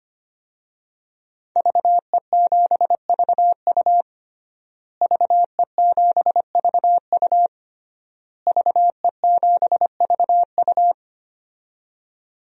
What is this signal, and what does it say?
Morse code, heard as a keyed tone, about 25 wpm, 700 Hz. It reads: VE7VU VE7VU VE7VU
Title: DXCC-100 Call Signs - 5 Characters - Repeated 3x - 25wpm